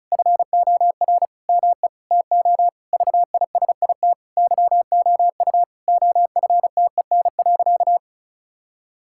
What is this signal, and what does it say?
Morse code, heard as a keyed tone, about 34 wpm, 700 Hz. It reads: FOR ME TO VISIT YOU OFTEN.
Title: Sentences from Top 1000 Words - Review Part 6 - 35wpm